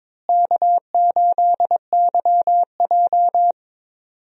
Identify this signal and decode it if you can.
Morse code, heard as a keyed tone, about 22 wpm, 700 Hz. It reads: K8YJ
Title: USA Call Signs - 4 Characters - Repeated 3x - 22wpm